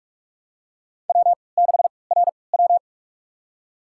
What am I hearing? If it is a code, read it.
Morse code, heard as a keyed tone, about 43 wpm, spaced as for 25 wpm, 700 Hz. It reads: W6RW
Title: USA Call Signs - 4 Characters - 3x Character Spacing - 45wpm